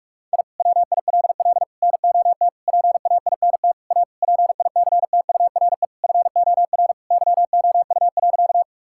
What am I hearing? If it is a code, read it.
Morse code, heard as a keyed tone, about 42 wpm, 700 Hz. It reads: I WILL NOT PAINT A PICTURE FOR YOU.
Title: Sentences from Top 1000 Words - Review Part 6 - 45wpm